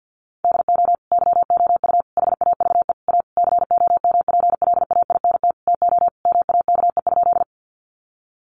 Morse code, heard as a keyed tone, about 48 wpm, 700 Hz. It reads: DO YOU HAVE A COMPLAINT TO MAKE?
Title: Sentences from Top 1400 Words 50wpm